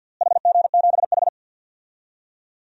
Morse code, heard as a keyed tone, about 47 wpm, 700 Hz. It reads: HG7H
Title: DXCC-100 Call Signs - 4 Characters - Repeated 3x - 50wpm